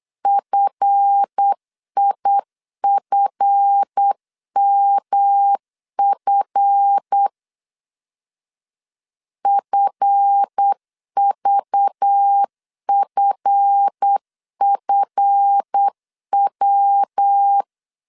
Morse code, heard as a keyed tone, about 8 wpm, 800 Hz. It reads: FIFMF FVFFW
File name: LEZIONI / 7 GF9Z / 7 GF 8wpm 3108